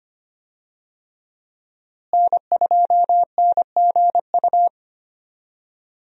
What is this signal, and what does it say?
Morse code, heard as a keyed tone, about 25 wpm, 700 Hz. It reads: N2NGU
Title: USA Call Signs - 5 Characters - Repeated 3x - 25wpm